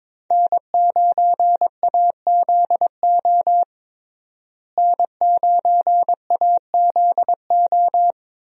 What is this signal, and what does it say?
Morse code, heard as a keyed tone, about 22 wpm, 700 Hz. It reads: N9AZO N9AZO
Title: USA Call Signs - 5 Characters - Repeated 2x - 22wpm